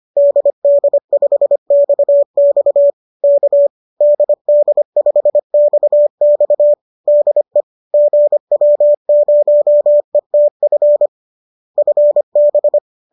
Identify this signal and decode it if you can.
Morse code, heard as a keyed tone, about 25 wpm, 570 Hz. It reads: DD5XX K DD5XX DE GW0ETF FB
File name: qso218 25